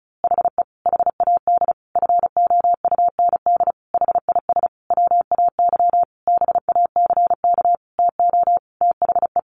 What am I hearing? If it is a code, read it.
Morse code, heard as a keyed tone, about 34 wpm, 700 Hz. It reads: HE HAD FOUND HIS WAY BACK TO THE
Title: A Sound Of Thunder 35wpm